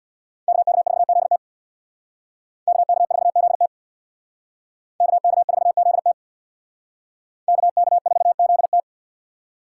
Morse code, heard as a keyed tone, about 62 wpm, 700 Hz. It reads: KK4ZT KK4ZT KK4ZT KK4ZT
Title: Speed Racing - USA Call Signs 50wpm